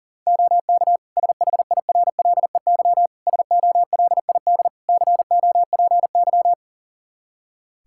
Morse code, heard as a keyed tone, about 40 wpm, 700 Hz. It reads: OK SHIRLEY SOLID COPY
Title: Practice QSO - Exchange 3 - More Info - 40wpm